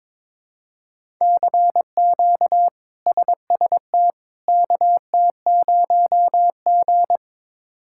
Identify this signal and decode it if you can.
Morse code, heard as a keyed tone, about 22 wpm, 700 Hz. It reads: CQ SST KT0G
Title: K1USN SST QSO Practice - 22wpm